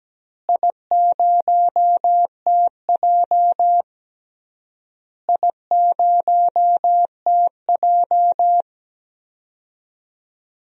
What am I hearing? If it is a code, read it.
Morse code, heard as a keyed tone, about 17 wpm, 700 Hz. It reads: I0TJ I0TJ
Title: DXCC-100 Call Signs - 4 Characters - Repeated 3x - 17wpm